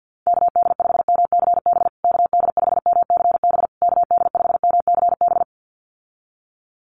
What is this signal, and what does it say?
Morse code, heard as a keyed tone, about 50 wpm, 700 Hz. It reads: KD5MCB KD5MCB KD5MCB
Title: USA Call Signs Repeated 3x 50wpm